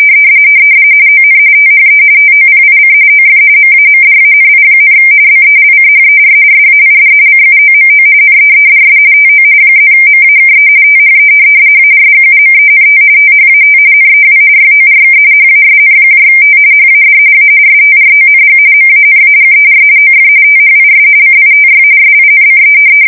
RTTY, about 45.45 baud. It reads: ON THE TOPIC OF MAMDANI, BELCHER SAID CUOMO NEEDS TO +MAKE THE ARGUMENT THAT NEW YORK HASNT HAD A STRONG MANAGER OF THIS CITY SINCE
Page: 170 Shift 2125/2295